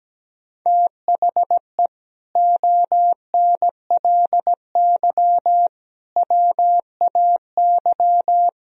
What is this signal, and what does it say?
Morse code, heard as a keyed tone, about 17 wpm, 700 Hz. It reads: THE ONLY WAY
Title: Speed-Racing - Sets of 4 Words 17wpm